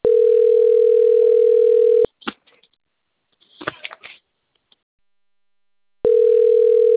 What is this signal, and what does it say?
Ringback tone (North America), 440 + 480 Hz, with speech.
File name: rg-1600-unknown- - - .33129